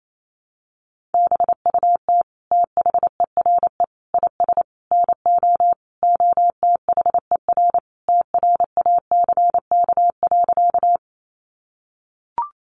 Morse code, heard as a keyed tone, about 28 wpm, 700 Hz. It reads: BUT THERE IS NO OTHER TRACK.
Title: Sentences from Top 1100 Words 28wpm